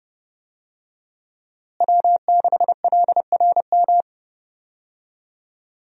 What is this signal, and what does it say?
Morse code, heard as a keyed tone, about 30 wpm, 700 Hz. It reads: W6LRM
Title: USA Call Signs - 5 Characters - Repeated 3x - 30wpm